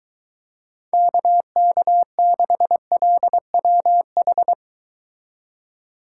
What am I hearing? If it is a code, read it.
Morse code, heard as a keyed tone, about 23 wpm, 700 Hz. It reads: KK6LWH